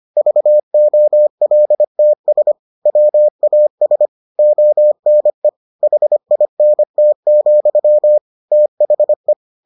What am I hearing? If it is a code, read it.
Morse code, heard as a keyed tone, about 25 wpm, 600 Hz. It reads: VOLTS WAS ONE HINT, THE